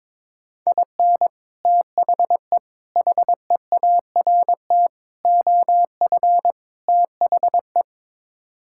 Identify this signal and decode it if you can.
Morse code, heard as a keyed tone, about 22 wpm, 700 Hz. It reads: IN THE HEART OF THE
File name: practice / 22wpm / Build A Phrase - 5-Word Phrases - 22wpm